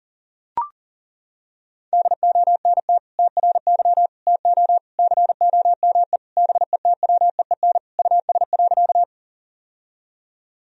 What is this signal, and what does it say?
Morse code, heard as a keyed tone, about 37 wpm, 700 Hz. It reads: DONT TRY TO COME BETWEEN US.
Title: Sentences from Top 1000 Words - Review Part 1 - 40wpm